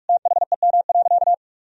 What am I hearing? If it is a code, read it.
Morse code, heard as a keyed tone, about 46 wpm, 700 Hz. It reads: THEM.
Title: Sentences from Top 1000 Words - Review Part 4 - 45wpm